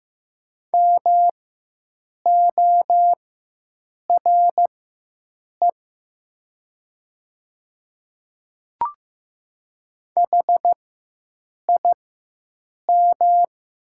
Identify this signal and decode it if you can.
Morse code, heard as a keyed tone, about 15 wpm, spaced as for 7 wpm, 700 Hz. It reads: MORE HIM
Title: 4x Character Spacing - Top 100 Words - 15wpm